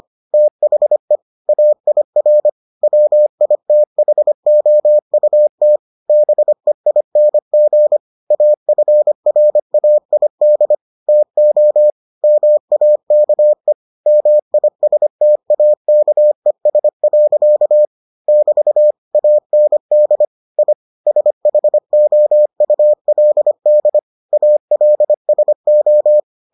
Morse code, heard as a keyed tone, about 25 wpm, 600 Hz. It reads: THE AIR WITHOUT BEING AFRAID TO MAKE MISTAKES. <BT> AND I SHOULD ALSO